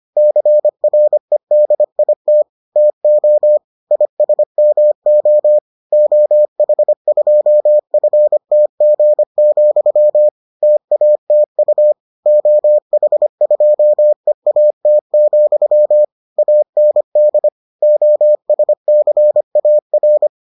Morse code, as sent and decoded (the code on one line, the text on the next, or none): -.-. .-. -.. .. - / - --- / .. ... -- --- / --- .... ..--- ..-. - --. --..-- / - .- - ..- / --- .... ..--- . .- - --..-- / .- -. -.. / --- ... -.-. .- .-.
CREDIT TO ISMO OH2FTG, TATU OH2EAT, AND OSCAR